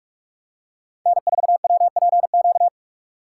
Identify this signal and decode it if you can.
Morse code, heard as a keyed tone, about 46 wpm, 700 Hz. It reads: N4WPQ
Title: USA Call Signs - 5 Characters - 2x Character Spacing - 45wpm